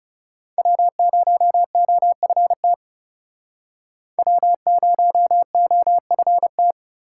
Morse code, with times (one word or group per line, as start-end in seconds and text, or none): 0.6-2.7 W0OFT
4.2-6.7 W0OFT